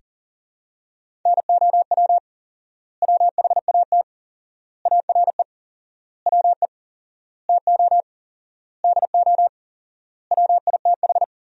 Morse code, heard as a keyed tone, about 40 wpm, 700 Hz. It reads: NOW WHAT ARE WE TO DO WITH
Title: 4x Word-Spacing - Sentences from Top 100 Words - 40wpm